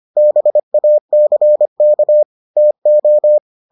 Morse code, heard as a keyed tone, about 25 wpm, 600 Hz. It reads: BACK TO